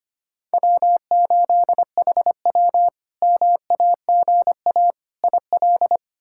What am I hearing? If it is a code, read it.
Morse code, heard as a keyed tone, about 25 wpm, 700 Hz. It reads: W8HW MAGA IL
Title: K1USN SST QSO Practice - 25wpm